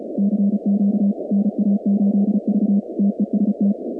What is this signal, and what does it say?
Morse code, heard as a keyed tone, about 36 wpm, 200 Hz. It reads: CQ NA8V TEST
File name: contests / 2019-cw-wpx-kt5j-k5pi- -2330 0-46